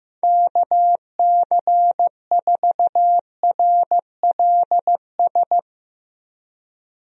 Morse code, heard as a keyed tone, about 15 wpm, 700 Hz. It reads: KC4RLS
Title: USA Call Signs - 6 Characters - Repeated 2x - 15wpm